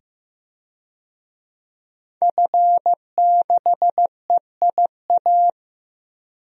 Morse code, heard as a keyed tone, about 15 wpm, 700 Hz. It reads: F6EIA
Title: DXCC-100 Call Signs - 5 Characters - 15wpm